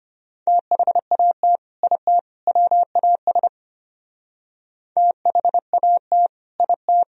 Morse code, heard as a keyed tone, about 32 wpm, 700 Hz. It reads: THAT IT WAS THAT IT